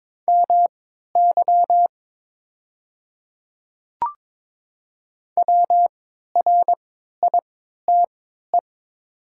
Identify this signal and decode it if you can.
Morse code, heard as a keyed tone, about 22 wpm, spaced as for 13 wpm, 700 Hz. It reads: MY WRITE
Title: 3x Character Spacing - Top 100 Words - 22wpm